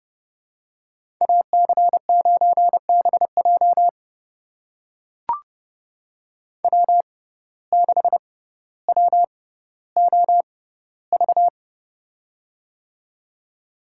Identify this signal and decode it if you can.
Morse code, heard as a keyed tone, about 31 wpm, spaced as for 16 wpm, 700 Hz. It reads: AC9BJ W6WOV
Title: USA Call Signs - 5 Characters - 6x Character Spacing - 30wpm